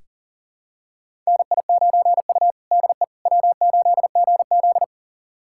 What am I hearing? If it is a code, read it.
Morse code, heard as a keyed tone, about 37 wpm, 700 Hz. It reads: NI9U DE W8GZ
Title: USA Callsign DE Callsign - 4 Characters - 40wpm